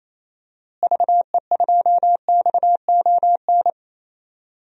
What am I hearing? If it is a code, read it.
Morse code, heard as a keyed tone, about 27 wpm, 700 Hz. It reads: VE2XON
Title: DXCC-100 Call Signs - 6 Characters - Repeated 2x - 28wpm